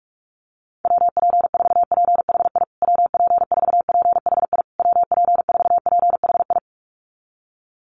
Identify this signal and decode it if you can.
Morse code, heard as a keyed tone, about 44 wpm, 700 Hz. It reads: WP4PHI WP4PHI WP4PHI
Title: USA Call Signs Repeated 3x 45wpm